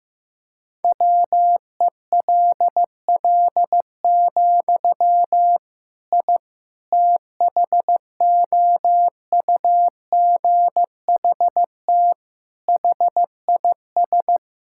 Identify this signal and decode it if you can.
Morse code, heard as a keyed tone, about 15 wpm, 700 Hz. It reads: WELL, I THOUGHT HIS
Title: Sentences from Top 1000 Words - Review Part 5 - 15wpm